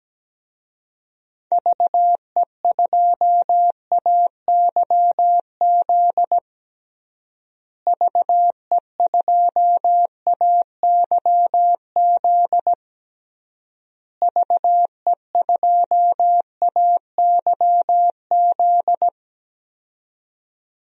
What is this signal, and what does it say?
Morse code, heard as a keyed tone, about 17 wpm, 700 Hz. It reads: VE2AYZ VE2AYZ VE2AYZ
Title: DXCC-100 Call Signs - 6 Characters - Repeated 3x - 17wpm